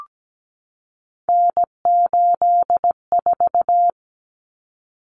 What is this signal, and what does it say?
Morse code, heard as a keyed tone, about 17 wpm, 700 Hz. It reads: N84
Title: Common Cut Numbers - 3-Digit 17wpm